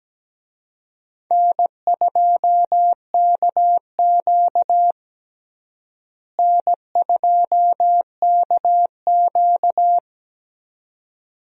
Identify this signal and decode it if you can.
Morse code, heard as a keyed tone, about 17 wpm, 700 Hz. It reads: N2KQ N2KQ
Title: USA Call Signs - 4 Characters - Repeated 2x - 17wpm